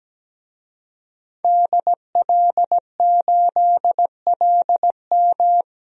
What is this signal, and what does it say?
Morse code, heard as a keyed tone, about 17 wpm, 700 Hz. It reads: DL8LM
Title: DXCC-100 Call Signs - 5 Characters - 17wpm